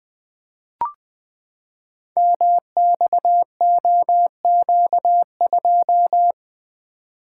Morse code, heard as a keyed tone, about 20 wpm, 700 Hz. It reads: MXOQ2
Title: Code Groups - 5 Letter:Numbers - 20wpm